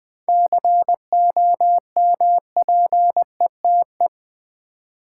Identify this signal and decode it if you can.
Morse code, heard as a keyed tone, about 21 wpm, 700 Hz. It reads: COMPETE
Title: 7 Letter Words - First 3 Letters Given - 20wpm